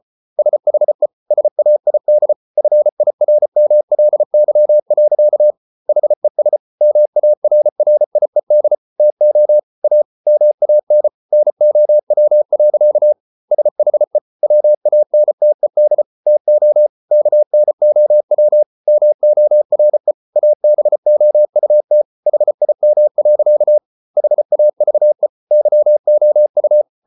Morse code, heard as a keyed tone, about 34 wpm, 600 Hz. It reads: SHE SAID FIRMLY. HES MARRIED TO A MAN NOW. SHE WANTED TO KNOW MORE ABOUT HIM. HAVE YOU